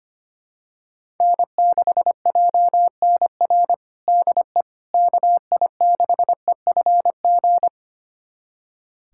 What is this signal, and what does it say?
Morse code, heard as a keyed tone, about 25 wpm, 700 Hz. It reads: N6JNR DE KI6EFG
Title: USA Callsign DE Callsign 25wpm